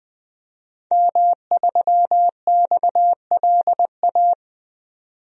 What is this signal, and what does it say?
Morse code, heard as a keyed tone, about 20 wpm, 700 Hz. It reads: M3XLA
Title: DXCC-100 Call Signs - 5 Characters - 20wpm